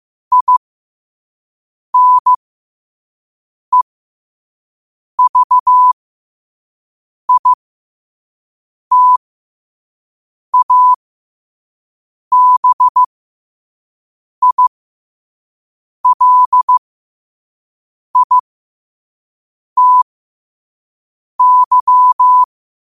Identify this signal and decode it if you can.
Morse code, heard as a keyed tone, about 15 wpm, spaced as for 5 wpm, 1000 Hz. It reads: INEVITABILITY
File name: qotdfiles / 5wpm